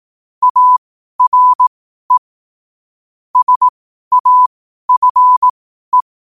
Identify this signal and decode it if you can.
Morse code, heard as a keyed tone, about 18 wpm, spaced as for 12 wpm, 1000 Hz. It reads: ARE SAFE
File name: qotdfiles / 12wpm